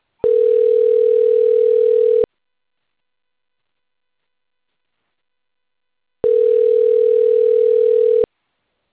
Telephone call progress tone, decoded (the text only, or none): ringback tone